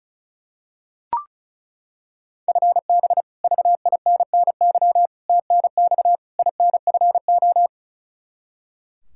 Morse code, heard as a keyed tone, about 34 wpm, 700 Hz. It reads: FB VINNY TNX INFO
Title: Practice QSO - Exchange 3 - More Info - 35wpm